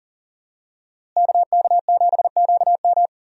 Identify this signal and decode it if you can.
Morse code, heard as a keyed tone, about 40 wpm, 700 Hz. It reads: KK7QM
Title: USA Call Signs - 5 Characters - Repeated 3x - 40wpm